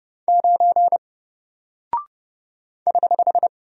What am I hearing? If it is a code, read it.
Morse code, heard as a keyed tone, about 30 wpm, 700 Hz. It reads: 9 <HH>
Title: Single Character (Letter:Number:Punctuation:Prosign) Alt 30wpm